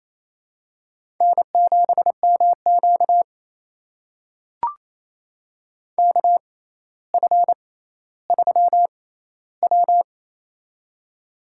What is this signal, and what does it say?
Morse code, heard as a keyed tone, about 28 wpm, spaced as for 14 wpm, 700 Hz. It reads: N7MQ KF3W
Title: USA Call Signs - 4 Characters - 6x Character Spacing - 28wpm